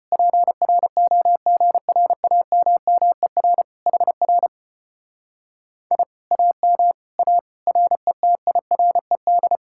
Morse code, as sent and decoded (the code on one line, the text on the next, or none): .--. .-. --- --. .-. .- -- -- . .-. / .... .-. / .. / .- -- / .- / .-. - .. .-. -..
PROGRAMMER HR I AM A RETIRED